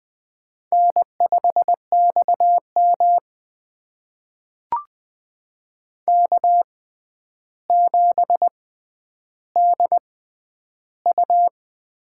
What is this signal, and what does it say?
Morse code, heard as a keyed tone, about 20 wpm, spaced as for 10 wpm, 700 Hz. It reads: N5XM K7DU
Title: USA Call Signs - 6x Character Spacing - 20wpm